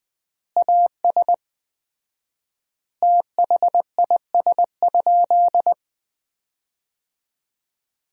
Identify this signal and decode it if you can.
Morse code, heard as a keyed tone, about 20 wpm, 700 Hz. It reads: AS THIS?
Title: 4x Word-Spacing - Sentences from Top 400 Words - 20wpm